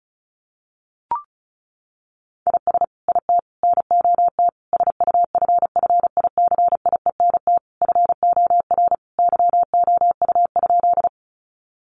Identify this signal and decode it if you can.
Morse code, heard as a keyed tone, about 34 wpm, 700 Hz. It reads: IS IT NOT SUFFICIENT FOR YOU?
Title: Sentences from Top 1400 Words 35wpm